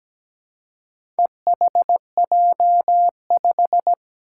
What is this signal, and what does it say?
Morse code, heard as a keyed tone, about 17 wpm, 700 Hz. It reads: EHJ5
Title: Four Challenging Letter-Number Combinations 17wpm